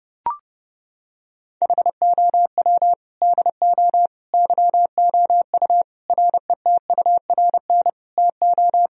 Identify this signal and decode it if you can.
Morse code, heard as a keyed tone, about 30 wpm, 700 Hz. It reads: HOW DO YOU RETURN TO
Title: Sentences from Top 1000 Words - Review Part 3 - 30wpm